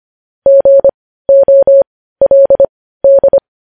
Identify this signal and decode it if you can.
Morse code, heard as a keyed tone, about 25 wpm, spaced as for 15 wpm, 550 Hz. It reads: GOLD